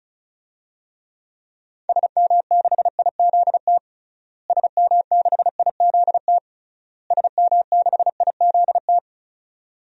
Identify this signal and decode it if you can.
Morse code, heard as a keyed tone, about 36 wpm, 700 Hz. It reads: SM6IZT SM6IZT SM6IZT
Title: DXCC-100 Call Signs - 6 Characters - Repeated 3x - 35wpm